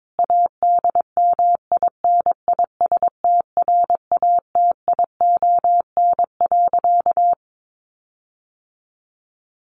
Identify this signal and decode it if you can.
Morse code, heard as a keyed tone, about 22 wpm, 700 Hz. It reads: ADMINISTRATION.
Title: Sentences from Top 1500 Words 22wpm